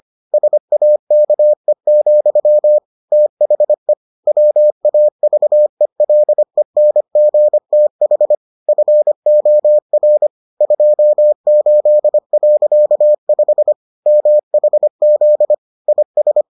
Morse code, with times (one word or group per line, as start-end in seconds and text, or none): 0.3-2.8 SAKE,
3.1-3.9 THE
4.3-8.3 WAVELENGTH
8.7-10.3 FOR
10.6-13.7 28.5
14.1-15.5 MHZ
15.9-16.4 IS